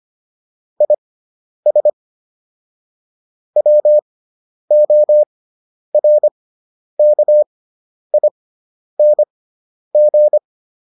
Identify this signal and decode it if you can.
Morse code, heard as a keyed tone, about 25 wpm, spaced as for 10 wpm, 600 Hz. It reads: IS WORKING